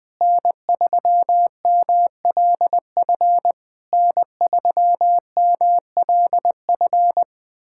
Morse code, heard as a keyed tone, about 21 wpm, 700 Hz. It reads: N3MLF N3MLF
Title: SOTA QSO Practice 20wpm